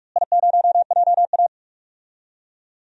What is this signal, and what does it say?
Morse code, heard as a keyed tone, about 45 wpm, 700 Hz. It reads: I0JA
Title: DXCC-100 Call Signs - 4 Characters - Repeated 3x - 45wpm